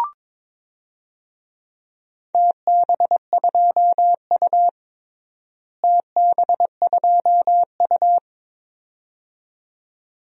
Morse code, heard as a keyed tone, about 22 wpm, 700 Hz. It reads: TB2U TB2U
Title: DXCC-100 Call Signs - 4 Characters - Repeated 2x - 22wpm